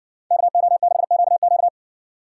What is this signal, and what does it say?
Morse code, heard as a keyed tone, about 60 wpm, 700 Hz. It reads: KO6QQ